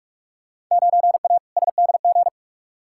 Morse code, heard as a keyed tone, about 46 wpm, 700 Hz. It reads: 9A SDG
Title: Field Day Exchanges Encore - 45wpm